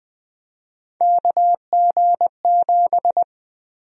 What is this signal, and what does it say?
Morse code, heard as a keyed tone, about 20 wpm, 700 Hz. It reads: KG7
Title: USA Call Sign Prefixes - 20wpm